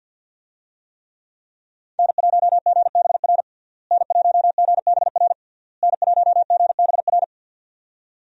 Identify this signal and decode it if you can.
Morse code, heard as a keyed tone, about 51 wpm, 700 Hz. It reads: N1GBR N1GBR N1GBR
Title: USA Call Signs - 5 Characters - Repeated 3x - 50wpm